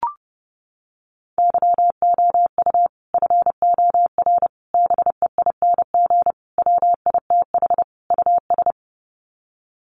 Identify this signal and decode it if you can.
Morse code, heard as a keyed tone, about 28 wpm, 700 Hz. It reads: YOU FOR BEING WITH US
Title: Sets of 5 Words 30wpm